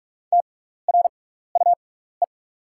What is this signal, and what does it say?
Morse code, heard as a keyed tone, about 45 wpm, spaced as for 15 wpm, 700 Hz. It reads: TRUE
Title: 6x Character Spacing - Top 500 Words - 45wpm